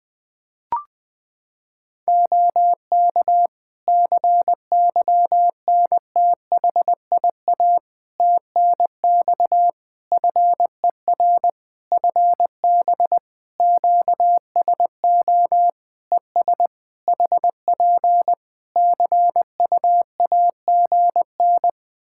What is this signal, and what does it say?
Morse code, heard as a keyed tone, about 20 wpm, 700 Hz. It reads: OK CYNTHIA TNX FER FB QSO ES HP CUAGN
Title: K1USN SST QSO Practice 20wpm